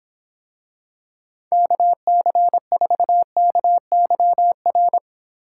Morse code, heard as a keyed tone, about 24 wpm, 700 Hz. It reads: KC4KYR